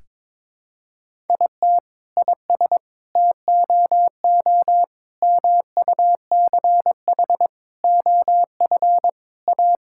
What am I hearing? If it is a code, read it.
Morse code, heard as a keyed tone, about 22 wpm, 700 Hz. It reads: IT IS TOO MUCH OF A
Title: Sentences from Top 1000 Words - Review Part 5 - 22wpm